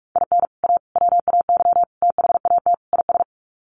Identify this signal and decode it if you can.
Morse code, heard as a keyed tone, about 44 wpm, 700 Hz. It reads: IN A WAY THAT IS
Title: Sets of 5 Words Encore 45wpm